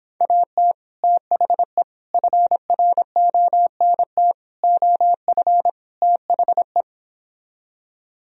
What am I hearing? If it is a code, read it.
Morse code, heard as a keyed tone, about 26 wpm, 700 Hz. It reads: AT THE FRONT OF THE